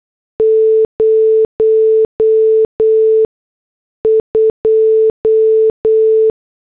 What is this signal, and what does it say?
Morse code, heard as a keyed tone, about 8 wpm, spaced as for 6 wpm, 440 Hz. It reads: 02